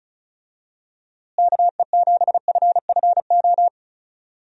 Morse code, heard as a keyed tone, about 36 wpm, 700 Hz. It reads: KE7FFO
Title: USA Call Signs - 6 Characters - Repeated 3x - 35wpm